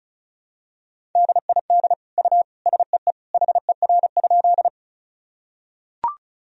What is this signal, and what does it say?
Morse code, heard as a keyed tone, about 36 wpm, 700 Hz. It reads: DID U SEE HER?
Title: Sentences from Top 100 Words and Abbreviations 35wpm